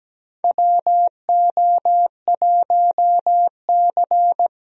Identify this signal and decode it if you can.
Morse code, heard as a keyed tone, about 17 wpm, 700 Hz. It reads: WO1C
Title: USA Call Signs - 4 Characters - Repeated 3x - 17wpm